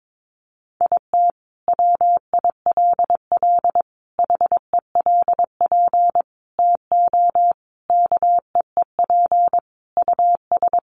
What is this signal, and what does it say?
Morse code, heard as a keyed tone, about 22 wpm, 700 Hz. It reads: IT WILL HELP TO KEEP US
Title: Sentences from Top 1400 Words 22wpm